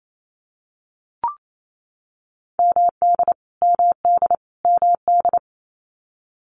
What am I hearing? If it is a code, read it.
Morse code, heard as a keyed tone, about 28 wpm, 700 Hz. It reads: MD MD MD
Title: US States Repeated 3x 28wpm